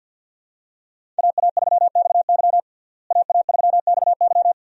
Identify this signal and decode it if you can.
Morse code, heard as a keyed tone, about 52 wpm, 700 Hz. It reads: AA3XY AA3XY
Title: USA Call Signs - 5 Characters - Repeated 2x - 50wpm